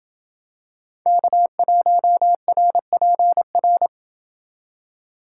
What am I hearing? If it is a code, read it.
Morse code, heard as a keyed tone, about 27 wpm, 700 Hz. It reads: K1RPR